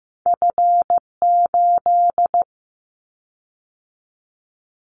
Morse code, heard as a keyed tone, about 15 wpm, 700 Hz. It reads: F8
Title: Callsign Prefixes for Top 25 DXCC - 15wpm